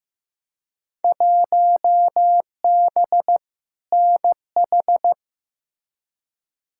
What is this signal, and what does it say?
Morse code, heard as a keyed tone, about 15 wpm, 700 Hz. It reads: 1B NH
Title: Field Day Exchanges Encore - 15wpm